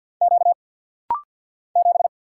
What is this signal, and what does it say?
Morse code, heard as a keyed tone, about 48 wpm, 700 Hz. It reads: Q 7
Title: Single Letter-Number - Rapid-Fire 50wpm